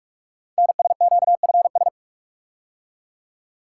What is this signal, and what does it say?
Morse code, heard as a keyed tone, about 44 wpm, 690 Hz. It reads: NSQFS
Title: Code Groups - 5 Letters - 45wpm